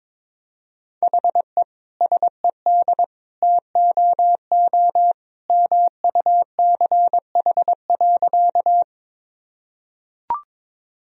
Morse code, heard as a keyed tone, about 23 wpm, 700 Hz. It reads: HE SED TOO MUCH.
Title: Sentences from Top 200 Words and Abbreviations 22wpm